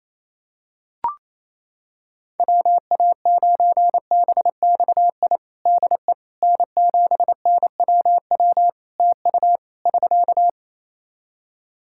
Morse code, heard as a keyed tone, about 28 wpm, 700 Hz. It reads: WA9BXI DE N7NWW TU <SK>
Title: Practice QSO - Exchange 4 - Ending - 28wpm